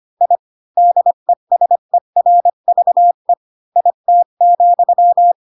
Morse code, heard as a keyed tone, about 25 wpm, 700 Hz. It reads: I DESERVE IT,
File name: Pinocchio / Pinocchio ch3 20 25 ik1hge